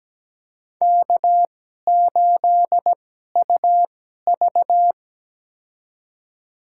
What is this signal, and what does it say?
Morse code, heard as a keyed tone, about 17 wpm, spaced as for 12 wpm, 700 Hz. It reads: K8UV
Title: USA Call Signs - 4 Characters - 2x Character Spacing - 17wpm